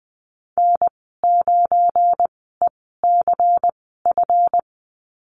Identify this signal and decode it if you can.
Morse code, heard as a keyed tone, about 20 wpm, spaced as for 15 wpm, 700 Hz. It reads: N9ECF
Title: 186 Lesson 30g - Letter b - Call Signs 20wpm